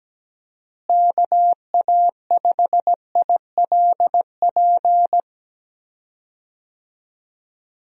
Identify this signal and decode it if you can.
Morse code, heard as a keyed tone, about 17 wpm, 700 Hz. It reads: KA5ILP
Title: USA Call Signs - 6 Characters - Repeated 2x - 17wpm